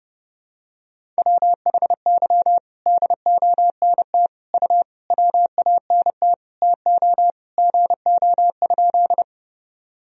Morse code, heard as a keyed tone, about 30 wpm, 700 Hz. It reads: WHY DONT U WANT TO GO?
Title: Sentences from Top 300 Words and Abbreviations 30wpm